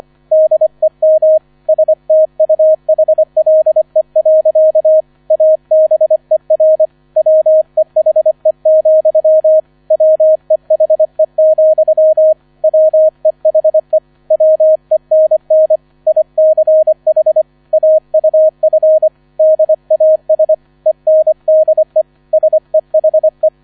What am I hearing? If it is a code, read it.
Morse code, heard as a keyed tone, about 25 wpm, 630 Hz, with radio noise. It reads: DEM STUHLE. ABER WEHE, WEHE, WEHE WENN ICH AUF DAS ENDE SEHE